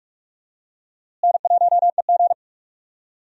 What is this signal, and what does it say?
Morse code, heard as a keyed tone, about 45 wpm, 700 Hz. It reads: N1EG
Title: USA Call Signs - 4 Characters - Repeated 3x - 45wpm